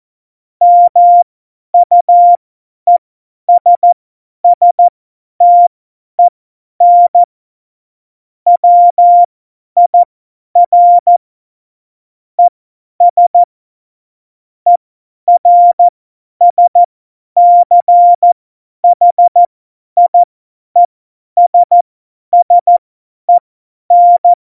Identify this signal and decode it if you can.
Morse code, heard as a keyed tone, about 14 wpm, spaced as for 10 wpm, 700 Hz. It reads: MUESSTEN WIR ES ERSCHIESSEN
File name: Morsekurs / Witze / Witz 26. Februar 2019, 50 F70